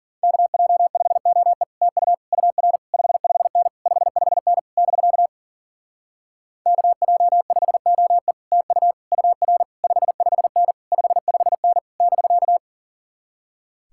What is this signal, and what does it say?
Morse code, heard as a keyed tone, about 52 wpm, 700 Hz. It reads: KJ5OE TU UR 55N 55N <BK> KJ5OE TU UR 55N 55N <BK>